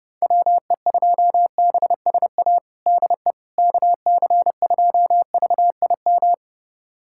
Morse code, heard as a keyed tone, about 30 wpm, 700 Hz. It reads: WE2BSA DE KC2VIM
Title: USA Callsign DE Callsign - 6 Characters - 30wpm